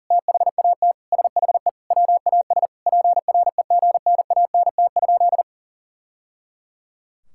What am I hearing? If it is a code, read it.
Morse code, heard as a keyed tone, about 40 wpm, 700 Hz. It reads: THAT SHE WAS PREGNANT?
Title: Sentences from Top 1000 Words - Review Part 6 - 40wpm